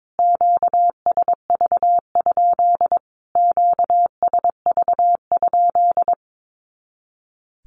Morse code, heard as a keyed tone, about 22 wpm, 700 Hz. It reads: QSV? QSV?
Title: Q-Codes Repeated 3x 22wpm